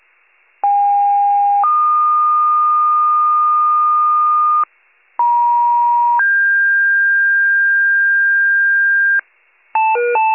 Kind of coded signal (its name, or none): two-tone page